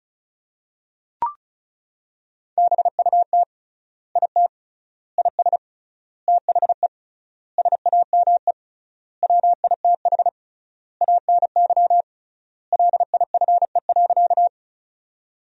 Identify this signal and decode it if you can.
Morse code, heard as a keyed tone, about 36 wpm, 700 Hz. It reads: BUT IT IS THE SAME WITH ANY LIFE.
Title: 3x Word-Spacing - Sentences from Top 300 Words - 35wpm